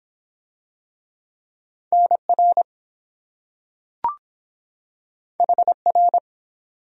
Morse code, heard as a keyed tone, about 23 wpm, 700 Hz. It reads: NR HR